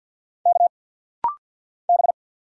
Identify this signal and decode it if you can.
Morse code, heard as a keyed tone, about 50 wpm, 690 Hz. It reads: K B
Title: Single Letter-Number Alt - Rapid-Fire 50wpm